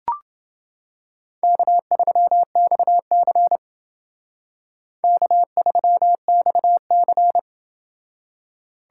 Morse code, heard as a keyed tone, about 28 wpm, 700 Hz. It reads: K3XC K3XC